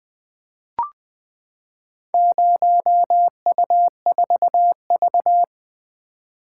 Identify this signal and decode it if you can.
Morse code, heard as a keyed tone, about 20 wpm, 700 Hz. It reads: 0U4V
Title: Four Letter-Number Combinations - 20wpm